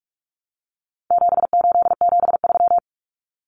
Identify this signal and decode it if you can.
Morse code, heard as a keyed tone, about 44 wpm, 700 Hz. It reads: 7873
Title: Common Cut Numbers - 4-Digit 45wpm